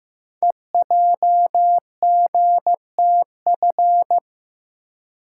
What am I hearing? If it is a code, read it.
Morse code, heard as a keyed tone, about 15 wpm, 700 Hz. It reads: EJGTF
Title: Code Groups - 5 Letters - 15wpm